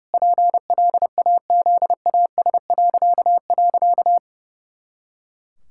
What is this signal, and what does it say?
Morse code, heard as a keyed tone, about 30 wpm, 700 Hz. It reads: PLAZAS..
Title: Pangram Challenge v2 - 30wpm